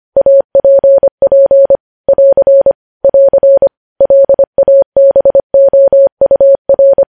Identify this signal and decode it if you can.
Morse code, heard as a keyed tone, about 25 wpm, 550 Hz. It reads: APP <AR> <AR> LABOUR
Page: news headlines in morse code at 25 wpm